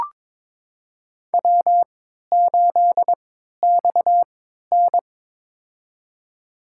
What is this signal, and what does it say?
Morse code, heard as a keyed tone, about 22 wpm, spaced as for 12 wpm, 700 Hz. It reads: W8XN
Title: USA Call Signs - 4 Characters - 3x Character Spacing - 22wpm